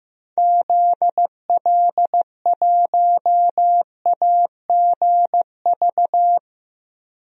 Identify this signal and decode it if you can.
Morse code, heard as a keyed tone, about 16 wpm, 700 Hz. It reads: ZL1AGV
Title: DXCC-100 Call Signs - 6 Characters - Repeated 3x - 15wpm